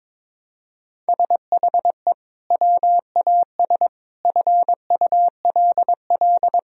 Morse code, heard as a keyed tone, about 22 wpm, 700 Hz. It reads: SHE WAS FULL
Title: Sentences from Top 1000 Words - Review Part 3 - 22wpm